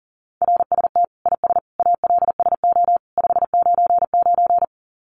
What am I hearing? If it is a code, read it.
Morse code, heard as a keyed tone, about 40 wpm, 700 Hz. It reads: RST IS ALSO 599
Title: Common QSO Phrases 40wpm